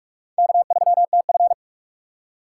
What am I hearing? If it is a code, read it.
Morse code, heard as a keyed tone, about 43 wpm, 700 Hz. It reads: K3TF
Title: K1USN SST QSO Practice - 45wpm